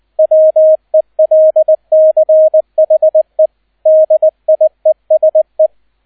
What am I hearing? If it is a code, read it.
Morse code, heard as a keyed tone, about 20 wpm, 630 Hz, with radio noise. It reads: WELCHE DIESE